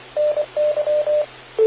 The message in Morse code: NY